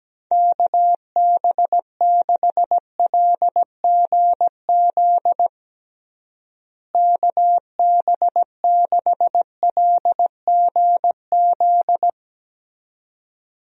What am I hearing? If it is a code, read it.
Morse code, heard as a keyed tone, about 17 wpm, 700 Hz. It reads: KB6LGZ KB6LGZ
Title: USA Call Signs - 6 Characters - Repeated 2x - 17wpm